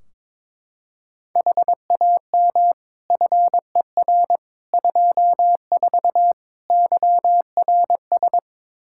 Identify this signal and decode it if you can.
Morse code, heard as a keyed tone, about 22 wpm, 700 Hz. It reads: HAM FER 24 YRS
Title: Practice QSO - Exchange 3 - More Info - 22wpm